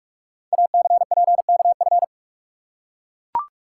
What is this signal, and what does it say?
Morse code, heard as a keyed tone, about 42 wpm, 700 Hz. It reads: ACPKF
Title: Code Groups - 5 Letter:Numbers - 45wpm